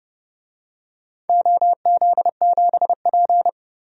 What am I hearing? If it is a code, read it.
Morse code, heard as a keyed tone, about 30 wpm, 700 Hz. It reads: OZ7P
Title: DXCC-100 Call Signs - 4 Characters - 30wpm